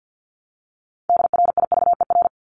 Morse code, heard as a keyed tone, about 50 wpm, 700 Hz. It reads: DRIVER
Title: Top 751-1000 Words 50wpm